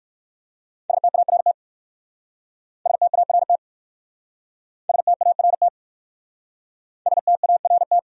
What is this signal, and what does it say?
Morse code, heard as a keyed tone, about 59 wpm, 700 Hz. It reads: START START START START